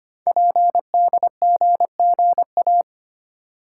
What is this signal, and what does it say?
Morse code, heard as a keyed tone, about 25 wpm, 700 Hz. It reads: PDGGA
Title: Code Groups - 5 Letters - 25wpm